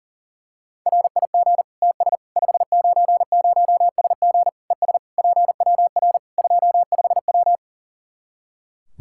Morse code, heard as a keyed tone, about 40 wpm, 700 Hz. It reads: RIG TS 590SG ES PWR 25W
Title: K1USN SST QSO Practice 40wpm